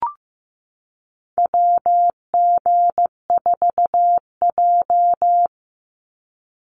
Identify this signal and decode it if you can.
Morse code, heard as a keyed tone, about 15 wpm, 700 Hz. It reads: WG4J
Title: USA Call Signs 15wpm